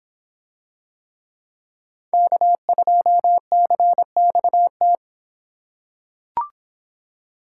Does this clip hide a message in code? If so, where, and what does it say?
Yes: Morse code from 2.1 to 5.0 s, heard as a keyed tone, about 26 wpm, 700 Hz. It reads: K2CXT